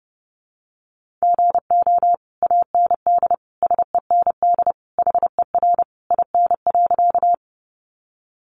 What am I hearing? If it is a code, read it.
Morse code, heard as a keyed tone, about 30 wpm, 700 Hz. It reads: GO AND SEND HER IN.
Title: Sentences from Top 1500 Words 30wpm